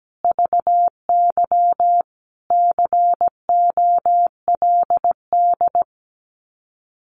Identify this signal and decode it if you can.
Morse code, heard as a keyed tone, about 17 wpm, 700 Hz. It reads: VY COLD
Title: Common QSO Phrases 17wpm